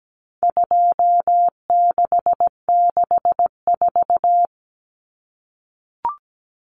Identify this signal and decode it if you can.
Morse code, heard as a keyed tone, about 17 wpm, 700 Hz. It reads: 2664
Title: Common Cut Numbers - 4-Digit 17wpm